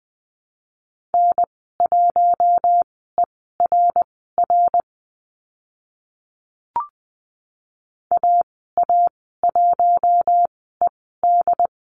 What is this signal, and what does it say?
Morse code, heard as a keyed tone, about 20 wpm, spaced as for 15 wpm, 700 Hz. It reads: N1ERR AA1ED
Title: 069 Lesson 12g - Letter d - Call Signs 20wpm